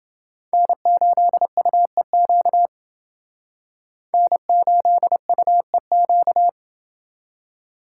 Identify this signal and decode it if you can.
Morse code, heard as a keyed tone, about 29 wpm, 700 Hz. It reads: N8UEQ N8UEQ